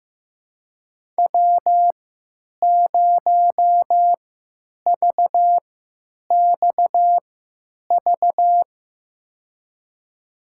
Morse code, heard as a keyed tone, about 15 wpm, spaced as for 9 wpm, 700 Hz. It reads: W0VXV
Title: USA Call Signs - 5 Characters - 3x Character Spacing - 15wpm